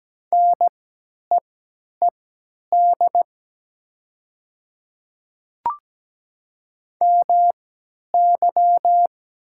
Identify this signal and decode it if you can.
Morse code, heard as a keyed tone, about 17 wpm, spaced as for 10 wpm, 700 Hz. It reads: NEED MY
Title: 3x Character Spacing - Top 300 Words - 17wpm